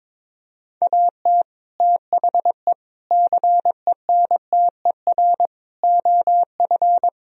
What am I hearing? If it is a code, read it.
Morse code, heard as a keyed tone, about 22 wpm, 700 Hz. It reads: AT THE CENTER OF
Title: Sentences from Top 1000 Words - Review Part 2 - 22wpm